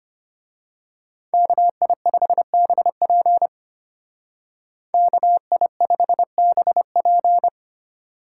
Morse code, heard as a keyed tone, about 28 wpm, 700 Hz. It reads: KI5BP KI5BP
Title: Speed Racing - USA 2x2 Call Signs 25wpm